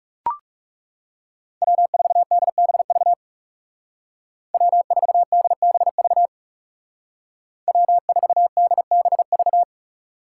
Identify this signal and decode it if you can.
Morse code, heard as a keyed tone, about 40 wpm, 700 Hz. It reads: W4DBV W4DBV W4DBV